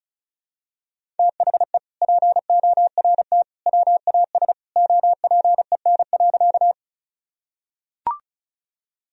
Morse code, heard as a keyed tone, about 34 wpm, 700 Hz. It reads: THE PORT WAS OPEN.
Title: Sentences from Top 1000 Words - Review Part 1 - 35wpm